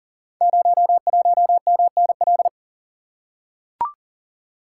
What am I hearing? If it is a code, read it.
Morse code, heard as a keyed tone, about 38 wpm, 700 Hz. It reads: 01MNL
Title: Code Groups - 5 Letter:Numbers - 40wpm